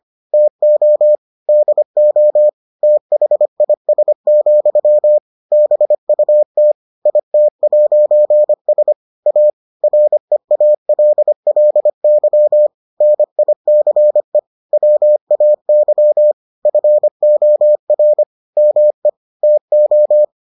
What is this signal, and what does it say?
Morse code, heard as a keyed tone, about 25 wpm, 600 Hz. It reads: TO DO THIS, BUT IT'S A REALLY NICE WAY FOR ME TO